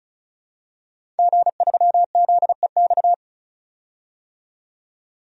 Morse code, heard as a keyed tone, about 36 wpm, 700 Hz. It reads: G3ZEX
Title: DXCC-100 Call Signs - 5 Characters - 35wpm